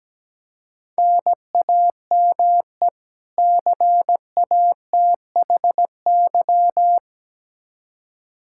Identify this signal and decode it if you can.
Morse code, heard as a keyed tone, about 17 wpm, 700 Hz. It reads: NAME CATHY
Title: Practice QSO - Exchange 1 - Beginning - 17wpm